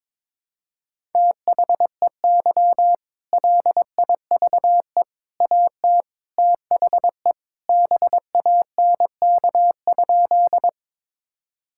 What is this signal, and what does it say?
Morse code, heard as a keyed tone, about 22 wpm, 700 Hz. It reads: THEY LIVE AT THE BANK?
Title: Sentences from Top 1000 Words - Review Part 6 - 22wpm